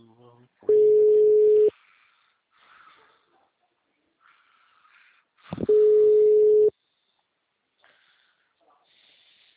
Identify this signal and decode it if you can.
Ringback tone (Europe, 425 Hz), with speech.
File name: out- -2007- - - .544